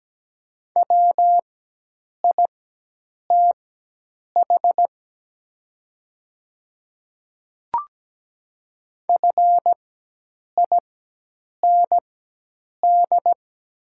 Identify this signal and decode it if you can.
Morse code, heard as a keyed tone, about 17 wpm, spaced as for 8 wpm, 700 Hz. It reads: WITH FIND
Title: 4x Character Spacing - Top 100 Words - 17wpm